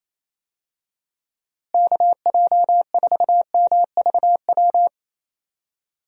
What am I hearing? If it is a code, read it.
Morse code, heard as a keyed tone, about 27 wpm, 700 Hz. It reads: KJ4MVW
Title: USA Call Signs - 6 Characters - Repeated 3x - 28wpm